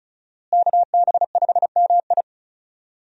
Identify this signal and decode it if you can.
Morse code, heard as a keyed tone, about 34 wpm, 700 Hz. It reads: KB5MI
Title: USA Call Signs - 5 Characters - Repeated 3x - 35wpm